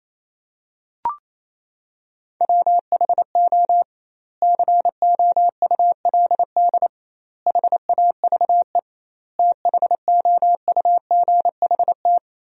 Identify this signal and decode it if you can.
Morse code, heard as a keyed tone, about 27 wpm, 700 Hz. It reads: WHO COULD HAVE THOUGHT
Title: 2x Word-Spacing - Sentences from Top 300 Words - 28wpm